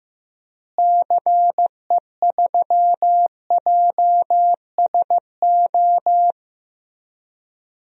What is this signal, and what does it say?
Morse code, heard as a keyed tone, about 15 wpm, 700 Hz. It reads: CE3JSO
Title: DXCC-100 Call Signs - 6 Characters - Repeated 2x - 15wpm